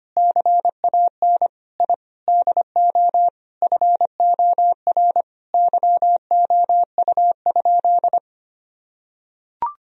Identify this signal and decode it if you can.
Morse code, heard as a keyed tone, about 25 wpm, 700 Hz. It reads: CAN I DO FOR YOU?
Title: Sentences from Top 1000 Words - Review Part 2 - 25wpm